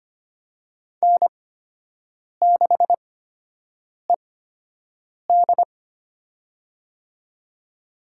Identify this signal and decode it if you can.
Morse code, heard as a keyed tone, about 25 wpm, spaced as for 7 wpm, 700 Hz. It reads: N6ED
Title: USA Call Signs - 4 Characters - 8x Character Spacing - 25wpm